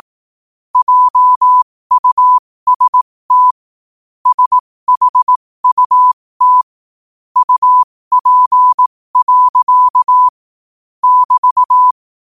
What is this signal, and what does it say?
Morse code, heard as a keyed tone, about 18 wpm, spaced as for 15 wpm, 1000 Hz. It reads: JUST SHUT UP. <BT>
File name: qotdfiles / 15wpm